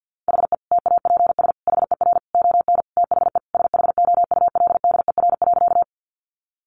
Morse code, heard as a keyed tone, about 50 wpm, 700 Hz. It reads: HE TAPS HER ON THE SHOULDER.
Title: Additional Easy Phrases 50wpm